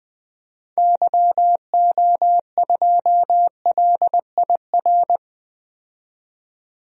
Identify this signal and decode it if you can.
Morse code, heard as a keyed tone, about 20 wpm, 700 Hz. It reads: YO2LIR
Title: DXCC-100 Call Signs - 6 Characters - 20wpm